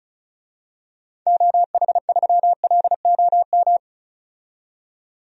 Morse code, heard as a keyed tone, about 34 wpm, 700 Hz. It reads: OH3LOM
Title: DXCC-100 Call Signs - 6 Characters - 35wpm